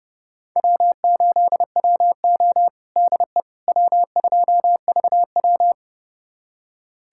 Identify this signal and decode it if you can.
Morse code, heard as a keyed tone, about 30 wpm, 700 Hz. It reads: W8WO DE W2VW
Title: USA Callsign DE Callsign - 4 Characters - 30wpm